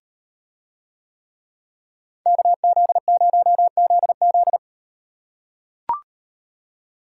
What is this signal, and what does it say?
Morse code, heard as a keyed tone, about 36 wpm, 700 Hz. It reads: KZ0ZZ